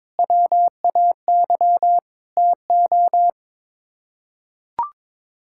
Morse code, heard as a keyed tone, about 22 wpm, 700 Hz. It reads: WAY TO
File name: practice / 22wpm / Sets of 8 Words - 22wpm